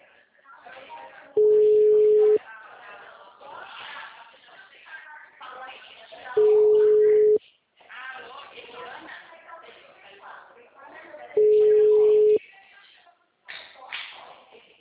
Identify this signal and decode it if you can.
Ringback tone (Europe, 425 Hz), with speech.
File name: out- -2010- - - .13465